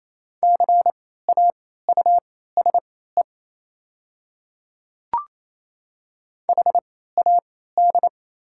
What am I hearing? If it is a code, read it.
Morse code, heard as a keyed tone, about 28 wpm, spaced as for 16 wpm, 700 Hz. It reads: CAUSE HAD
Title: 3x Character Spacing - Top 200 Words - 28wpm